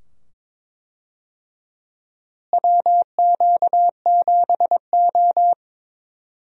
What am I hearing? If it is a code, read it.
Morse code, heard as a keyed tone, about 22 wpm, 700 Hz. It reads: WQ7O
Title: USA Call Signs - 4 Characters - Repeated 2x - 22wpm